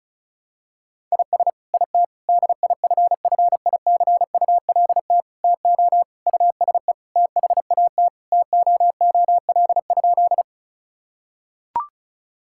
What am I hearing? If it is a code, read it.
Morse code, heard as a keyed tone, about 36 wpm, 700 Hz. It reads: IS IT DIFFICULT TO USE THAT TOOL?
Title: Sentences from Top 1000 Words - Review Part 6 - 35wpm